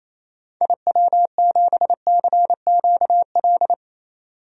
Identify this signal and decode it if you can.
Morse code, heard as a keyed tone, about 28 wpm, 700 Hz. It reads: IW7CQL
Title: DXCC-100 Call Signs - 6 Characters - 28wpm